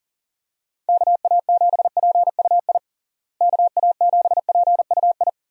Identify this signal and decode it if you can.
Morse code, heard as a keyed tone, about 40 wpm, 700 Hz. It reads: KA7PUI KA7PUI
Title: USA Call Signs - 6 Characters - Repeated 3x - 40wpm